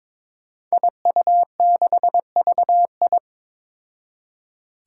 Morse code, heard as a keyed tone, about 22 wpm, 700 Hz. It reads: IU6VI
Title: Five Challenging Letter-Number Combinations 22wpm